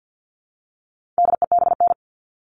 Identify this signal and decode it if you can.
Morse code, heard as a keyed tone, about 50 wpm, 700 Hz. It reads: DEBN
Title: All Cut Numbers - 4-Digit 50wpm